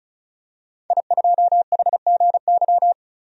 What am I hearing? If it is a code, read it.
Morse code, heard as a keyed tone, about 35 wpm, 700 Hz. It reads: I2HGY
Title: DXCC-100 Call Signs - 5 Characters - 35wpm